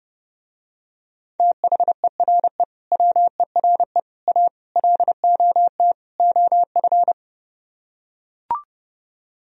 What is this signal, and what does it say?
Morse code, heard as a keyed tone, about 30 wpm, 700 Hz. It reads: THERE WERE A LOT OF